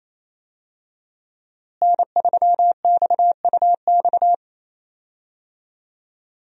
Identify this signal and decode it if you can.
Morse code, heard as a keyed tone, about 28 wpm, 700 Hz. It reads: N3XUX
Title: USA Call Signs - 5 Characters - Repeated 2x - 28wpm